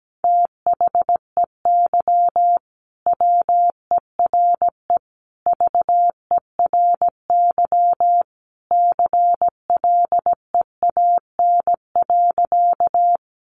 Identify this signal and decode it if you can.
Morse code, heard as a keyed tone, about 17 wpm, 700 Hz. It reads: THEY WERE VERY CLEAN.
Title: Sentences from Top 1100 Words 17wpm